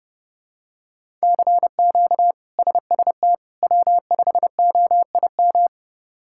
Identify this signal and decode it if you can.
Morse code, heard as a keyed tone, about 32 wpm, 700 Hz. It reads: CQ SST W5OIM
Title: K1USN SST QSO Practice - 30wpm